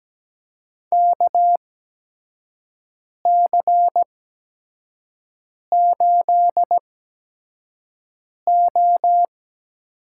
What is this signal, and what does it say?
Morse code, heard as a keyed tone, about 17 wpm, spaced as for 5 wpm, 700 Hz. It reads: KC8O
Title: USA Call Signs - 6 Characters - 8x Character Spacing - 17wpm